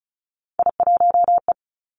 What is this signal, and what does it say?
Morse code, heard as a keyed tone, about 36 wpm, 700 Hz. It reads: I1E
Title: Three Challenging Letter-Number Combinations 35wpm